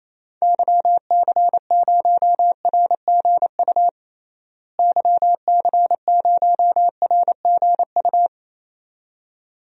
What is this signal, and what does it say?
Morse code, heard as a keyed tone, about 28 wpm, 700 Hz. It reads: YC0RGU YC0RGU
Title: DXCC-100 Call Signs - 6 Characters - Repeated 2x - 28wpm